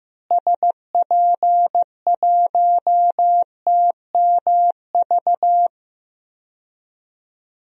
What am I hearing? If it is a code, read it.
Morse code, heard as a keyed tone, about 15 wpm, 700 Hz. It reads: SP1TMV
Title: DXCC-100 Call Signs - 6 Characters - Repeated 2x - 15wpm